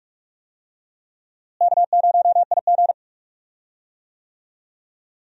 Morse code, heard as a keyed tone, about 41 wpm, 700 Hz. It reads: K0IG
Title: USA Call Signs - 4 Characters - Repeated 2x - 45wpm